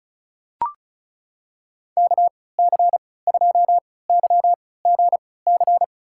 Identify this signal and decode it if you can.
Morse code, heard as a keyed tone, about 35 wpm, spaced as for 20 wpm, 700 Hz. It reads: KC2YGC
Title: USA Call Signs - 6 Characters - 3x Character Spacing - 35wpm